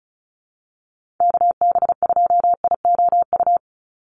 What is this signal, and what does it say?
Morse code, heard as a keyed tone, about 34 wpm, 700 Hz. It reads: KB2IOU
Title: USA Call Signs 35wpm